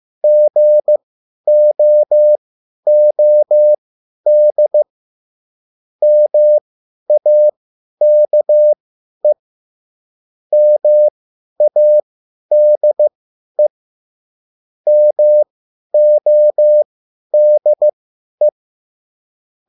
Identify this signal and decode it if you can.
Morse code, heard as a keyed tone, about 15 wpm, spaced as for 10 wpm, 600 Hz. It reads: GOOD MAKE MADE MODE
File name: cwbook